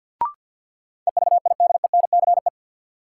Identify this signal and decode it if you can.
Morse code, heard as a keyed tone, about 53 wpm, 700 Hz. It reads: EVIDENCE
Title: Sets of 5 Words Encore - New Word Drill - 50wpm